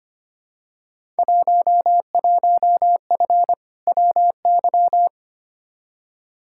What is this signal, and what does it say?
Morse code, heard as a keyed tone, about 25 wpm, 700 Hz. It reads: 11F WY
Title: Field Day Exchanges Encore - 25wpm